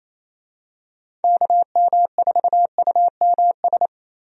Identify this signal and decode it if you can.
Morse code, heard as a keyed tone, about 28 wpm, 700 Hz. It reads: KM4UMS
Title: USA Call Signs - 6 Characters - Repeated 2x - 28wpm